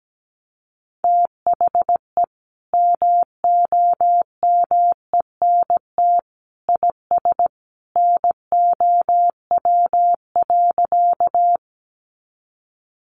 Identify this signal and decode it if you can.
Morse code, heard as a keyed tone, about 17 wpm, 700 Hz. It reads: THE MOMENT IS NOW.
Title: Sentences from Top 1000 Words 17wpm